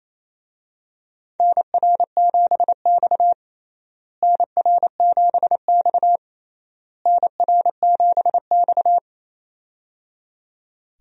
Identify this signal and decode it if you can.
Morse code, heard as a keyed tone, about 28 wpm, 700 Hz. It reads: NR7X NR7X NR7X
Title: USA Call Signs - 4 Characters - Repeated 3x - 28wpm